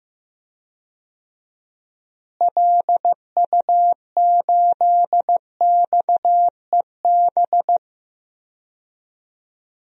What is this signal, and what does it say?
Morse code, heard as a keyed tone, about 15 wpm, 700 Hz. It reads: LU8XEB
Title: DXCC-100 Call Signs - 6 Characters - Repeated 2x - 15wpm